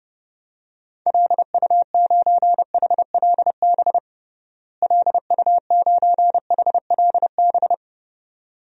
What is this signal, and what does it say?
Morse code, heard as a keyed tone, about 30 wpm, 700 Hz. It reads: LU9HLB LU9HLB
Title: DXCC-100 Call Signs - 6 Characters - Repeated 2x - 30wpm